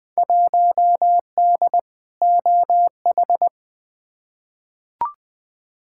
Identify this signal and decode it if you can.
Morse code, heard as a keyed tone, about 20 wpm, 700 Hz. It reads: 1D OH
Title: Field Day Exchanges - 20wpm